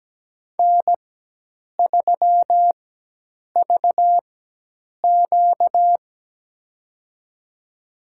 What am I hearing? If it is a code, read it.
Morse code, heard as a keyed tone, about 17 wpm, spaced as for 8 wpm, 700 Hz. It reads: N3VQ
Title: USA Call Signs - 4 Characters - 4x Character Spacing - 17wpm